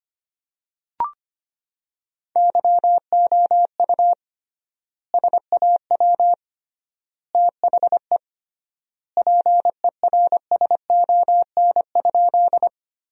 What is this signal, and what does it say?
Morse code, heard as a keyed tone, about 25 wpm, 700 Hz. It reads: YOU SAW THE PERSON?
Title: 3x Word-Spacing - Sentences from Top 400 Words - 25wpm